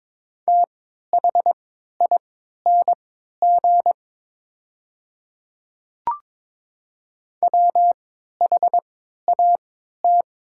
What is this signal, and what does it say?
Morse code, heard as a keyed tone, about 22 wpm, spaced as for 12 wpm, 700 Hz. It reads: THING WHAT
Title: 3x Character Spacing - Top 200 Words - 22wpm